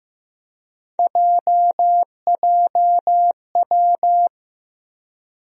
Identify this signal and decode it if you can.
Morse code, heard as a keyed tone, about 15 wpm, 700 Hz. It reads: JJW
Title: USA Call Sign Postfixes - 3 Letters 15wpm